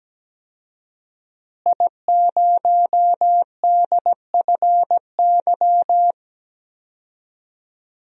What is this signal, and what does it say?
Morse code, heard as a keyed tone, about 17 wpm, 700 Hz. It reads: I0DFY
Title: DXCC-100 Call Signs - 5 Characters - Repeated 3x - 17wpm